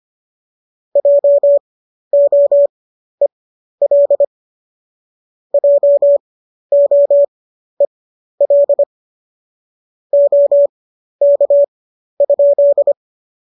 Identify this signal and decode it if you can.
Morse code, heard as a keyed tone, about 25 wpm, spaced as for 12 wpm, 570 Hz. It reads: JOEL JOEL OK?
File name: qso009 12